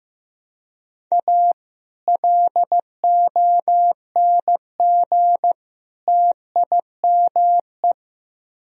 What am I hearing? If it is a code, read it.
Morse code, heard as a keyed tone, about 15 wpm, 700 Hz. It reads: A LONG TIME
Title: Speed-Racing - Sets of 3 Words 15wpm